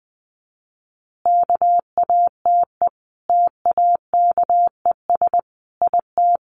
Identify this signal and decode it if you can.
Morse code, heard as a keyed tone, about 20 wpm, 700 Hz. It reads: KATE TAKES IT
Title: Easy Phrases 20wpm